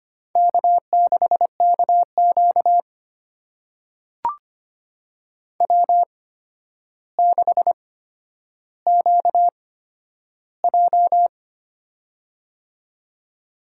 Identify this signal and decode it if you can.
Morse code, heard as a keyed tone, about 25 wpm, spaced as for 11 wpm, 700 Hz. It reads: K6KQ W6QJ
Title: USA Call Signs - 4 Characters - 8x Character Spacing - 25wpm